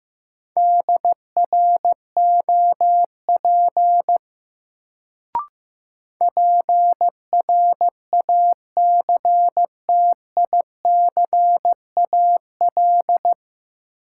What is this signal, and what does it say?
Morse code, heard as a keyed tone, about 15 wpm, 700 Hz. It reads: DROP PRACTICAL
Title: Sentences from Top 1400 Words - New Word Drill Alt - 15wpm